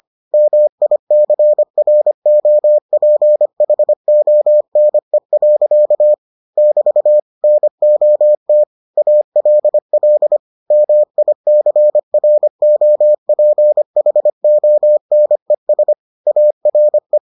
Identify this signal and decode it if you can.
Morse code, heard as a keyed tone, about 25 wpm, 600 Hz. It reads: MICROPHONE. <BT> NOT ALL MICROPHONES ARE